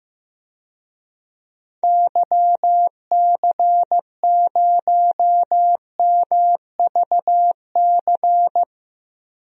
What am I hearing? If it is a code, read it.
Morse code, heard as a keyed tone, about 16 wpm, 700 Hz. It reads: YC0MVC
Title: DXCC-100 Call Signs - 6 Characters - 15wpm